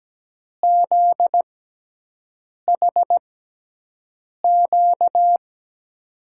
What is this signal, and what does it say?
Morse code, heard as a keyed tone, about 17 wpm, spaced as for 6 wpm, 700 Hz. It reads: ZHQ
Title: USA Call Signs - 6 Characters - 6x Character Spacing - 17wpm